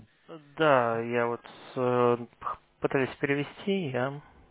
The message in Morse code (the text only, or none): none